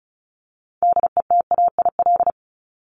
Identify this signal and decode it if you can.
Morse code, heard as a keyed tone, about 34 wpm, 700 Hz. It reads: DETAIL
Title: Top 1251-1500 Words 35wpm